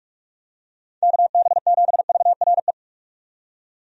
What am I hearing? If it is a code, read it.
Morse code, heard as a keyed tone, about 45 wpm, 700 Hz. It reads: KB7VRE
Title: USA Call Signs - 6 Characters - 45wpm